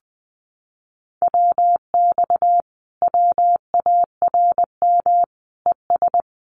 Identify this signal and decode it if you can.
Morse code, heard as a keyed tone, about 20 wpm, 700 Hz. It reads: WX WARM ES
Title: Common QSO Phrases Encore 20wpm